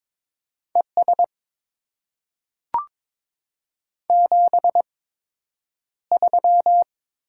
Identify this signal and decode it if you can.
Morse code, heard as a keyed tone, about 22 wpm, spaced as for 9 wpm, 700 Hz. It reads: ES 73
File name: Top 20 QSO Elements - 8x Character Spacing - 22wpm